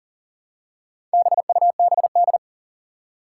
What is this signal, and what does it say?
Morse code, heard as a keyed tone, about 40 wpm, 700 Hz. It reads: BUBD
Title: Four Challenging Letter-Number Combinations 40wpm